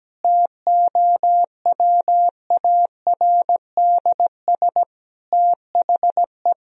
Morse code, heard as a keyed tone, about 17 wpm, 700 Hz. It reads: TOWARDS THE
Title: Sentences from Top 1000 Words - Review Part 2 - 17wpm